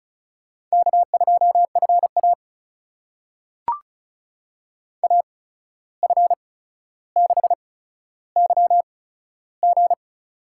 Morse code, heard as a keyed tone, about 36 wpm, spaced as for 14 wpm, 700 Hz. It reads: K2FA AF6YG
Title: USA Call Signs - 8x Character Spacing - 35wpm